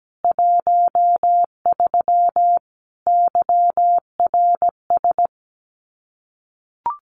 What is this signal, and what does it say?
Morse code, heard as a keyed tone, about 17 wpm, 700 Hz. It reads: 13 YRS
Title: Common QSO Phrases 17wpm